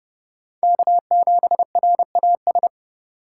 Morse code, heard as a keyed tone, about 29 wpm, 700 Hz. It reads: K7RAS
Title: USA Call Signs - 5 Characters - 2x Character Spacing - 30wpm